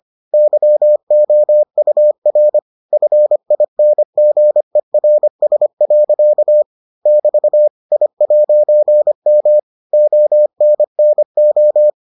Morse code, heard as a keyed tone, about 25 wpm, 600 Hz. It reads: YOUR FINGERS. <BT> I'M ONNO